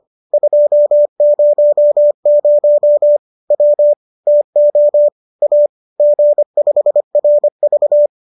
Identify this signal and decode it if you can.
Morse code, heard as a keyed tone, about 25 wpm, 590 Hz. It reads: 200 W TO A G5RV
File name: qso204 25